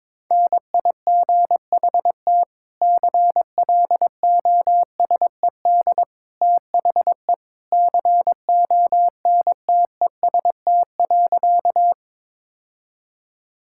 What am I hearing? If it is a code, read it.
Morse code, heard as a keyed tone, about 22 wpm, 700 Hz. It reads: NIGHT CLOSED THE CONTEST.
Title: Sentences from Top 1600 Words 22wpm